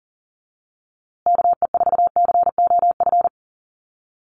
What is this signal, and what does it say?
Morse code, heard as a keyed tone, about 40 wpm, 700 Hz. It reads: KE4COF
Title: USA Call Signs Repeated 3x 40wpm